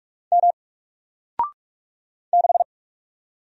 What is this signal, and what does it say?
Morse code, heard as a keyed tone, about 44 wpm, 700 Hz. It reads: M 6
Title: Single Letter-Number Alt 45wpm